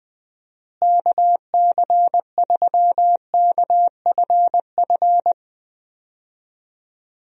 Morse code, heard as a keyed tone, about 20 wpm, 700 Hz. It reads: KC3KFF
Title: USA Call Signs - 6 Characters - Repeated 2x - 20wpm